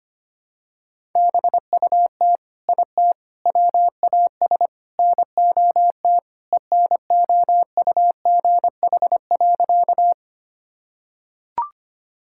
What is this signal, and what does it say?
Morse code, heard as a keyed tone, about 25 wpm, 700 Hz. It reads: BUT IT WAS NOT ENOUGH.
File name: practice / 25wpm / Sentences with 4-Word Phrases - 25wpm